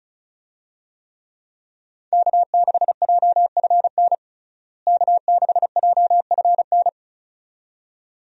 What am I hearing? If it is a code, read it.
Morse code, heard as a keyed tone, about 34 wpm, 700 Hz. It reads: K6JFN K6JFN
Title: USA Call Signs - 5 Characters - Repeated 2x - 35wpm